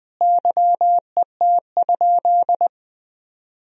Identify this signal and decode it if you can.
Morse code, heard as a keyed tone, about 20 wpm, 700 Hz. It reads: YET?
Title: Sentences from Top 1000 Words - Review Part 4 - 20wpm